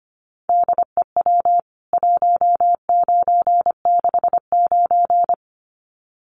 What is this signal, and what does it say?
Morse code, heard as a keyed tone, about 25 wpm, 700 Hz. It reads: DEW 1969
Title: CWT Exchange 25wpm